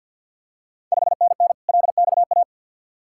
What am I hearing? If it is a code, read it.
Morse code, heard as a keyed tone, about 52 wpm, 700 Hz. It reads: 5NN LXA
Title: Ontario QSO Party Exchange 50wpm